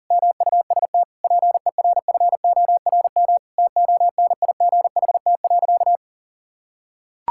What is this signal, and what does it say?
Morse code, heard as a keyed tone, about 40 wpm, 690 Hz. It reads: MUST PERFORM TONIGHT.
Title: Sentences from Top 1000 Words - Review Part 5 - 40wpm